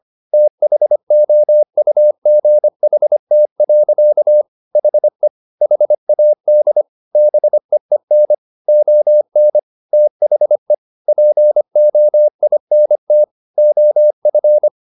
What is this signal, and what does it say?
Morse code, heard as a keyed tone, about 25 wpm, 600 Hz. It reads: THOUGHT. HE HAD BEEN ON THE POINT OF